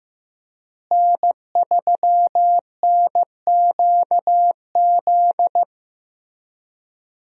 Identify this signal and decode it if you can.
Morse code, heard as a keyed tone, about 15 wpm, 700 Hz. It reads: N3NQZ
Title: USA Call Signs - 5 Characters - Repeated 3x - 15wpm